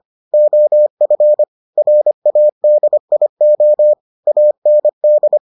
Morse code, heard as a keyed tone, about 25 wpm, 600 Hz. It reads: OF RADIO AND